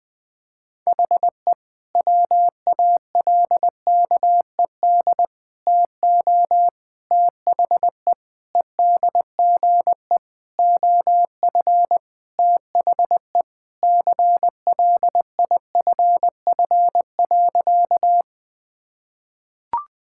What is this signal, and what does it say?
Morse code, heard as a keyed tone, about 21 wpm, 700 Hz. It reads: HE WALKED TO THE EDGE OF THE CLIFF.
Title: Sentences with 5-Word Phrases - 20wpm